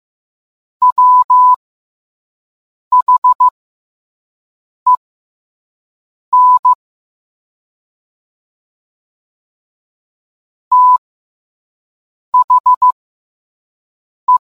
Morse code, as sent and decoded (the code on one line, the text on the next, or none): .-- .... -. / - .... .
WHEN THE